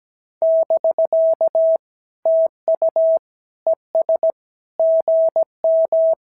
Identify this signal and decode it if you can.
Morse code, heard as a keyed tone, about 17 wpm, 650 Hz. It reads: <BK> TU ES GM
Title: SOTA QSO Practice 17wpm